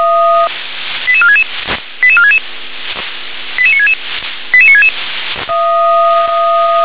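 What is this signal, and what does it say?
Five-tone selective call (ZVEI), heard as quick runs of five tones, tones of about 70 ms. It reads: 88466 88466 88966 88966
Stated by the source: Received by SDR, ZVEI five-tone.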